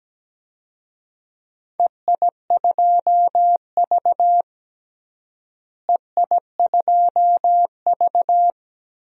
Morse code, heard as a keyed tone, about 17 wpm, 700 Hz. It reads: EI2V EI2V
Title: DXCC-100 Call Signs - 4 Characters - Repeated 3x - 17wpm